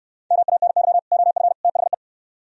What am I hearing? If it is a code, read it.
Morse code, heard as a keyed tone, about 67 wpm, 700 Hz. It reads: MANY OF THE